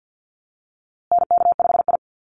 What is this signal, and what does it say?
Morse code, heard as a keyed tone, about 50 wpm, 700 Hz. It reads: NK5I
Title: USA Call Signs Repeated 2x 50wpm